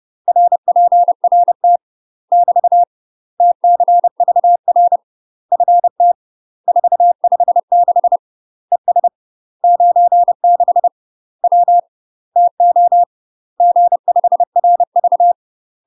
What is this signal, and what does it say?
Morse code, heard as a keyed tone, about 30 wpm, 700 Hz. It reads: RPRT <BT> TCVR FT 456 ES 96 W TO G5RV
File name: UK MASTER QSOs TRAINING / 30WPM / UK MASTER QSOs 0020